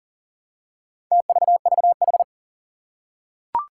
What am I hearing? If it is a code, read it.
Morse code, heard as a keyed tone, about 44 wpm, 700 Hz. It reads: TVVH
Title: Four Challenging Letter-Number Combinations 40wpm